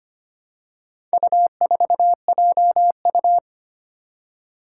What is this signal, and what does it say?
Morse code, heard as a keyed tone, about 25 wpm, 700 Hz. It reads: U4JU
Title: Four Challenging Letter-Number Combinations 25wpm